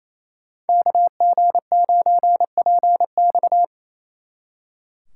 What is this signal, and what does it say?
Morse code, heard as a keyed tone, about 34 wpm, 700 Hz. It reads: KG9PX
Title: Speed Racing - USA 2x2 Call Signs 28wpm